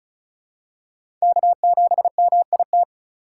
Morse code, heard as a keyed tone, about 34 wpm, 700 Hz. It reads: K7MIT
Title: USA Call Signs - 5 Characters - 2x Character Spacing - 35wpm